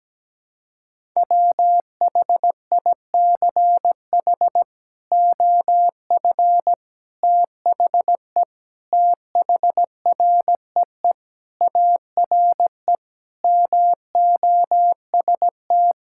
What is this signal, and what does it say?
Morse code, heard as a keyed tone, about 17 wpm, 700 Hz. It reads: WHICH OF THE THREE ARE MOST
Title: Sentences from Top 1000 Words - Review Part 5 - 17wpm